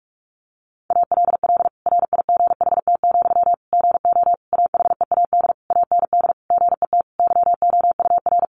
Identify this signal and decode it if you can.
Morse code, heard as a keyed tone, about 44 wpm, 700 Hz. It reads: ALL RIGHT, GO AHEAD AND GET YOUR
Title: Sentences from Top 1100 Words 45wpm